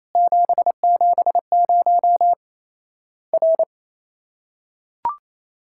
Morse code, heard as a keyed tone, about 28 wpm, 700 Hz. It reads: 770 R
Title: Worked All Europe DX Contest - QTC Exchange - 28wpm